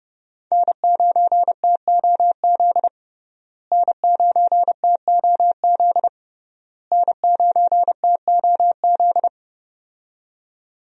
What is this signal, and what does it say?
Morse code, heard as a keyed tone, about 30 wpm, 700 Hz. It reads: N9TOZ N9TOZ N9TOZ
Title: USA Call Signs - 5 Characters - Repeated 3x - 30wpm